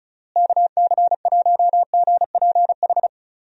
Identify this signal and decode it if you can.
Morse code, heard as a keyed tone, about 34 wpm, 700 Hz. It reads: KC1GPH
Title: USA Call Signs - 6 Characters - Repeated 3x - 35wpm